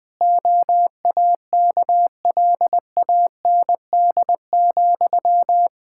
Morse code, heard as a keyed tone, about 20 wpm, 700 Hz. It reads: OAKLAND,
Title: K1USN SST QSO Practice 20wpm